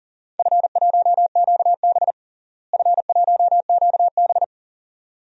Morse code, heard as a keyed tone, about 37 wpm, 700 Hz. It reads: F1QB F1QB
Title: DXCC-100 Call Signs - 4 Characters - Repeated 2x - 40wpm